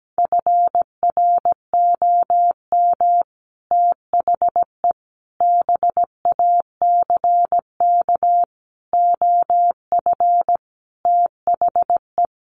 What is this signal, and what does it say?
Morse code, heard as a keyed tone, about 17 wpm, 700 Hz. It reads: FROM THE BACK OF THE
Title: Sets of 5 Words Encore 17wpm